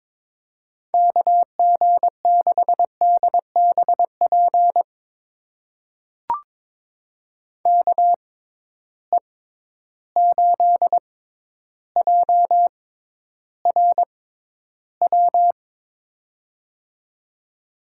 Morse code, heard as a keyed tone, about 22 wpm, spaced as for 11 wpm, 700 Hz. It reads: KG6DBP KE8JRW
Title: USA Call Signs - 6x Character Spacing - 22wpm